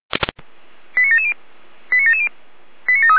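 Five-tone selective call (ZVEI), heard as quick runs of five tones, tones of about 70 ms. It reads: 89770 89770 89735
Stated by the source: ZVEI five-tone call received by SDR, from audio.f85360.feuerwehr-orlamuende.org (Feuerwehr Orlamünde fire service)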